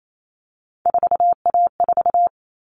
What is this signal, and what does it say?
Morse code, heard as a keyed tone, about 28 wpm, 700 Hz. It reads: 4A4
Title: All Cut Numbers - 3-Digit 28wpm